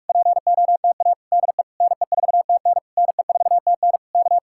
Morse code, heard as a keyed tone, about 59 wpm, spaced as for 52 wpm, 700 Hz. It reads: POTA DE NE4TN NE4TN K
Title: Speed-Racing Common QSO Phrases Encore - 45wpm